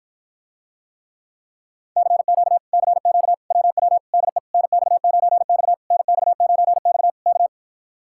Morse code, heard as a keyed tone, about 53 wpm, 700 Hz. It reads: CQ CQ WW DE NX9X NX9X K